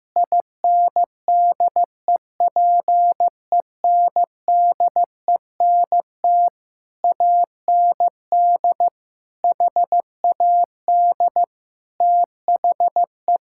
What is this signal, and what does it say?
Morse code, heard as a keyed tone, about 15 wpm, 700 Hz. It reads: INDEPENDENT AND HAD THE
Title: Sentences from Top 1000 Words - Review Part 6 - 15wpm